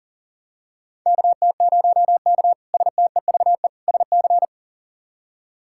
Morse code, heard as a keyed tone, about 37 wpm, 700 Hz. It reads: KT0K STEVE SC
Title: K1USN SST QSO Practice - 40wpm